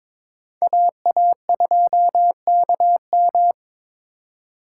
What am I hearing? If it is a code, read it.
Morse code, heard as a keyed tone, about 26 wpm, 700 Hz. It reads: AA2KM
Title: Speed Racing - USA Call Signs 22wpm